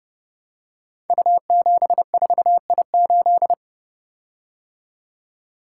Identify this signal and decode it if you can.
Morse code, heard as a keyed tone, about 28 wpm, 700 Hz. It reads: U74I8
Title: Code Groups - 5 Letter:Numbers - 30wpm